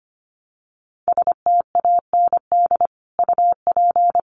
Morse code, heard as a keyed tone, about 25 wpm, 700 Hz. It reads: STAND UP
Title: Sentences from Top 1000 Words 25wpm